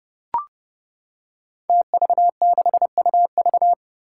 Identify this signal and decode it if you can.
Morse code, heard as a keyed tone, about 28 wpm, 700 Hz. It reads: TV6UV
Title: Five Challenging Letter-Number Combinations 30wpm